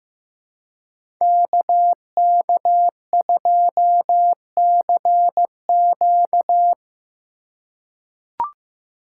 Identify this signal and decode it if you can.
Morse code, heard as a keyed tone, about 15 wpm, 700 Hz. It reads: KK2CQ
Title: USA Call Signs - 5 Characters - 2x Character Spacing - 15wpm